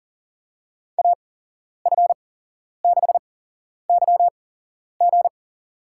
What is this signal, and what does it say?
Morse code, heard as a keyed tone, about 40 wpm, spaced as for 11 wpm, 700 Hz. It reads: AF6YG
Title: USA Call Signs - 5 Characters - 8x Character Spacing - 40wpm